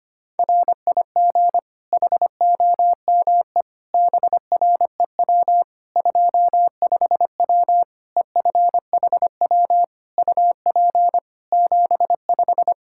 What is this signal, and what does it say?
Morse code, heard as a keyed tone, about 25 wpm, 700 Hz. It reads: RIG HOME BREW 25W EFHW UP 75
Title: K1USN SST QSO Practice 25wpm